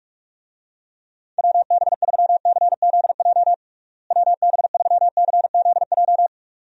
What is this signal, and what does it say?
Morse code, heard as a keyed tone, about 46 wpm, 700 Hz. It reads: WB3CZJ WB3CZJ
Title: USA Call Signs - 6 Characters - Repeated 2x - 45wpm